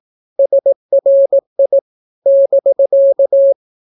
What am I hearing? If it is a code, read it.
Morse code, heard as a keyed tone, about 18 wpm, 550 Hz. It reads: SRI <BK>
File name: naukapodstaw / skroty / skroty